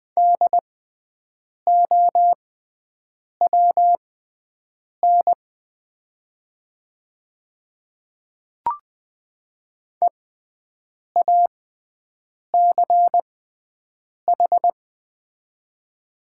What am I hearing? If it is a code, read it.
Morse code, heard as a keyed tone, about 20 wpm, spaced as for 7 wpm, 700 Hz. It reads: DOWN EACH
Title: 6x Character Spacing - Top 200 Words - 20wpm